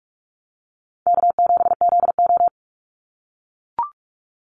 Morse code, heard as a keyed tone, about 44 wpm, 700 Hz. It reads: K7ZO
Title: NA SOTA Chaser Callsigns 45wpm